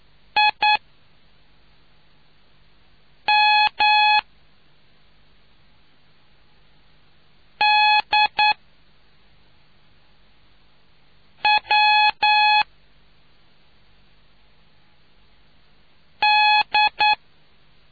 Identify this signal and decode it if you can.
Morse code, heard as a keyed tone, about 9 wpm, spaced as for 3 wpm, 840 Hz. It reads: IMDWD